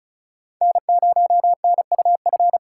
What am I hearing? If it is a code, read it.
Morse code, heard as a keyed tone, about 34 wpm, 700 Hz. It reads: N0NUF
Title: USA Call Signs - 5 Characters - 3x Character Spacing - 35wpm